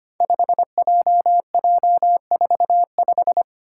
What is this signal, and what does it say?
Morse code, heard as a keyed tone, about 25 wpm, 700 Hz. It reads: 5JJ45
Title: Five Challenging Letter-Number Combinations 25wpm